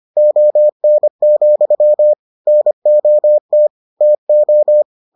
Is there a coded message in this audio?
Yes: Morse code from 0.2 to 4.8 s, heard as a keyed tone, about 25 wpm, 600 Hz. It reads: ON, NOT TO